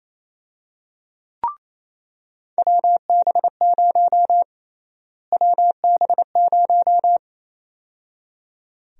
Morse code, heard as a keyed tone, about 28 wpm, 700 Hz. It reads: WB0 WB0
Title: USA Call Sign Prefixes - Repeated 2x - 28wpm